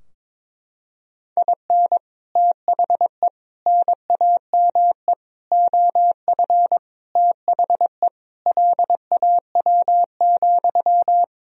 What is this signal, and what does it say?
Morse code, heard as a keyed tone, about 22 wpm, 700 Hz. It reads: IN THE NAME OF THE LAW,
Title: Sentences from Top 1000 Words - Review Part 3 - 22wpm